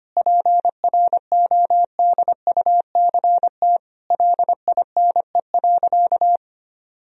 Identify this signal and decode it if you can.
Morse code, heard as a keyed tone, about 25 wpm, 700 Hz. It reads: PRODUCT LINE.
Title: Sentences from Top 1000 Words - Review Part 3 - 25wpm